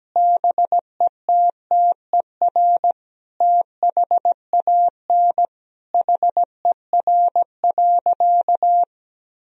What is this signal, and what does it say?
Morse code, heard as a keyed tone, about 17 wpm, 700 Hz. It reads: BETTER THAN HER.
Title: Sentences from Top 1000 Words - Review Part 2 - 17wpm